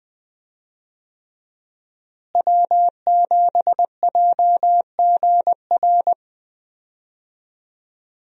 Morse code, heard as a keyed tone, about 20 wpm, 700 Hz. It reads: W7JGR
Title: USA Call Signs - 5 Characters - Repeated 3x - 20wpm